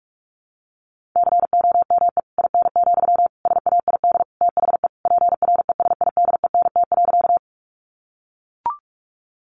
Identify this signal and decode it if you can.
Morse code, heard as a keyed tone, about 44 wpm, 700 Hz. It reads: COME IN, SAID THE PRESIDENT.
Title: Sentences from Top 1000 Words 45wpm